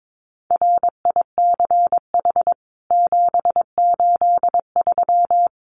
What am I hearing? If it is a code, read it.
Morse code, heard as a keyed tone, about 22 wpm, 700 Hz. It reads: RICH 783
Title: CWT Exchange 22wpm